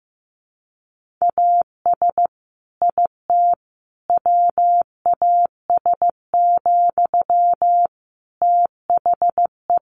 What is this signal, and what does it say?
Morse code, heard as a keyed tone, about 15 wpm, 700 Hz. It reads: AS IT WAS, THE
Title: Sentences from Top 1500 Words 15wpm